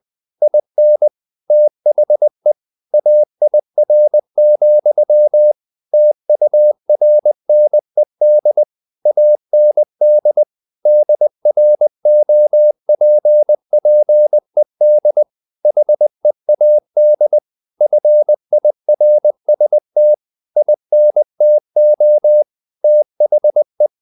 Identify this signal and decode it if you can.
Morse code, heard as a keyed tone, about 20 wpm, 600 Hz. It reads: IN THE AIR, TURNED AND DROPPED HEAD FIRST INTO THE
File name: the old man and the sea / the old man and the sea part 1 20wpm